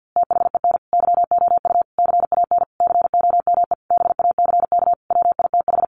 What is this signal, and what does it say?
Morse code, heard as a keyed tone, about 50 wpm, 700 Hz. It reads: THEN YOU CAN COME BACK WITH
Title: A Sound Of Thunder 50wpm